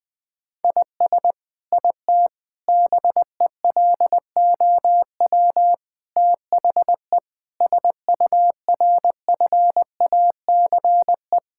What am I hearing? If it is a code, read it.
Morse code, heard as a keyed tone, about 20 wpm, 700 Hz. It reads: IS IT BELOW THE SURFACE
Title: Sentences from Top 1000 Words - Review Part 5 - 20wpm